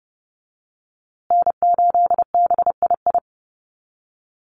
Morse code, heard as a keyed tone, about 30 wpm, 700 Hz. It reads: N8BII
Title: USA Call Signs Repeated 2x 30wpm